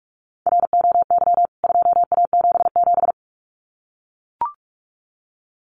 Morse code, heard as a keyed tone, about 44 wpm, 700 Hz. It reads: ROY 2A77
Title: CWT Exchange 45wpm